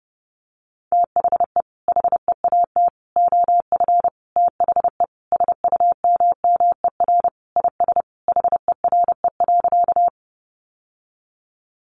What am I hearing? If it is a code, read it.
Morse code, heard as a keyed tone, about 30 wpm, 700 Hz. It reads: THE HEAT OF THE SUMMER IS HERE.
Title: Sentences from Top 1100 Words 30wpm